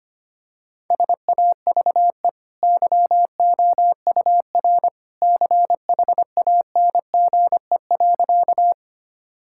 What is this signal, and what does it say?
Morse code, heard as a keyed tone, about 25 wpm, 700 Hz. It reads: SAVE YOUR CHANGE.
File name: Easy Sentences with Hints - 25wpm